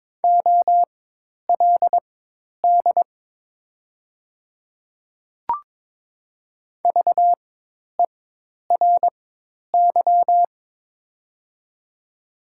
Morse code, heard as a keyed tone, about 22 wpm, spaced as for 10 wpm, 700 Hz. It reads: OLD VERY
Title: 4x Character Spacing - Top 300 Words - 22wpm